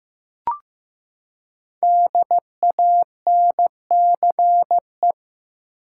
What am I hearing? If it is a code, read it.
Morse code, heard as a keyed tone, about 15 wpm, 700 Hz. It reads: DANCE
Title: Five Letter Words - 15wpm